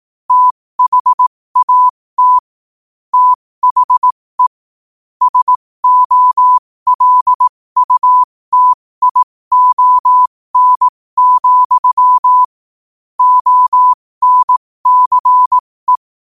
Morse code, heard as a keyed tone, about 18 wpm, spaced as for 15 wpm, 1000 Hz. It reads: THAT THE SOLUTION, ONCE
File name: qotdfiles / 15wpm